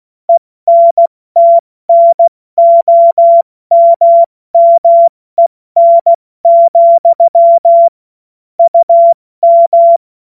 Morse code, heard as a keyed tone, about 16 wpm, 680 Hz. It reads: ENTNOMMEN, UM
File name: Morsekurs / Witze / Witz 21.November 2023 70 F80